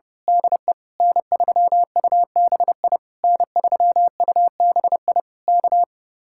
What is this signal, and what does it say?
Morse code, heard as a keyed tone, about 29 wpm, 700 Hz. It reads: DE N3UBI N3UBI K
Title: Speed-Racing Common QSO Phrases Encore - 30wpm